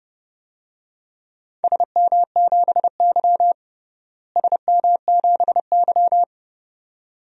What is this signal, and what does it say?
Morse code, heard as a keyed tone, about 30 wpm, 700 Hz. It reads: SM7Y SM7Y
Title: DXCC-100 Call Signs - 4 Characters - Repeated 2x - 30wpm